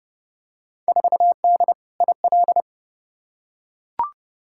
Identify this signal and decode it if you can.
Morse code, heard as a keyed tone, about 30 wpm, 700 Hz. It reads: 4D IL
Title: Field Day Exchanges Encore - 30wpm